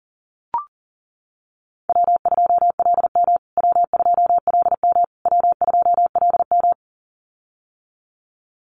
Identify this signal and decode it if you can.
Morse code, heard as a keyed tone, about 40 wpm, 700 Hz. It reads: W2LM W2LM W2LM
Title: USA Call Signs Repeated 3x 40wpm